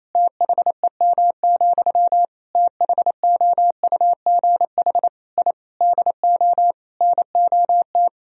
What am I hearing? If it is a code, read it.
Morse code, heard as a keyed tone, about 28 wpm, 700 Hz. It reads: THEM, THOUGH I DO NOT
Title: Sentences from Top 1000 Words - Review Part 2 - 28wpm